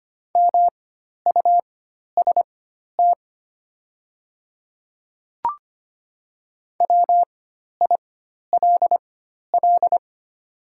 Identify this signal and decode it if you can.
Morse code, heard as a keyed tone, about 25 wpm, spaced as for 12 wpm, 700 Hz. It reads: MUST WILL
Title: 4x Character Spacing - Top 200 Words - 25wpm